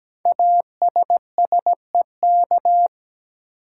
Morse code, heard as a keyed tone, about 17 wpm, 700 Hz. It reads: ASSEK
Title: Code Groups - 5 Letters - 17wpm